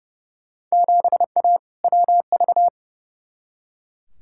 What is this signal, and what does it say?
Morse code, heard as a keyed tone, about 29 wpm, 700 Hz. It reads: 7A WV
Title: Field Day Exchanges Encore - 30wpm